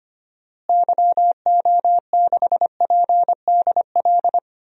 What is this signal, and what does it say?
Morse code, heard as a keyed tone, about 25 wpm, 700 Hz. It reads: YO6PDL
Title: DXCC-100 Call Signs - 6 Characters - Repeated 2x - 25wpm